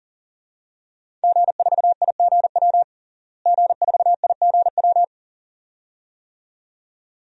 Morse code, heard as a keyed tone, about 37 wpm, 700 Hz. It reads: G4IGW G4IGW
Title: DXCC-100 Call Signs - 5 Characters - Repeated 2x - 40wpm